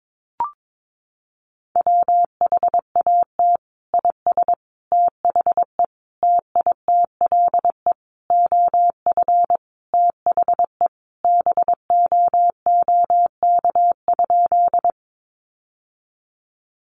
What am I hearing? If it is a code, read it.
Morse code, heard as a keyed tone, about 22 wpm, 700 Hz. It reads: WHAT IS THE TITLE OF THE BOOK?
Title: Sentences from Top 1200 Words 22wpm